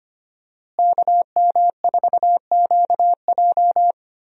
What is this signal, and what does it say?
Morse code, heard as a keyed tone, about 25 wpm, 700 Hz. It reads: KM4QJ
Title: USA Call Signs - 5 Characters - Repeated 2x - 25wpm